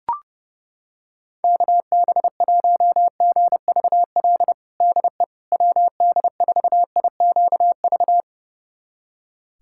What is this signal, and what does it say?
Morse code, heard as a keyed tone, about 30 wpm, 700 Hz. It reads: KB1GVL DE WD4IQV
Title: USA Callsign DE Callsign - 6 Characters - 30wpm